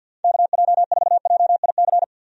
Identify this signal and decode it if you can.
Morse code, heard as a keyed tone, about 48 wpm, 700 Hz. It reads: KJ4JIC
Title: USA Call Signs - 6 Characters - 50wpm